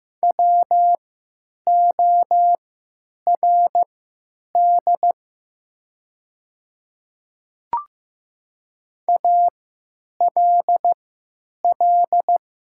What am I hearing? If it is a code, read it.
Morse code, heard as a keyed tone, about 15 wpm, spaced as for 9 wpm, 700 Hz. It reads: WORD ALL
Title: 3x Character Spacing - Top 100 Words - 15wpm